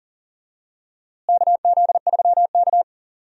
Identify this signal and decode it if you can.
Morse code, heard as a keyed tone, about 40 wpm, 700 Hz. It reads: KZ3K
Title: USA Call Signs - 4 Characters - Repeated 3x - 40wpm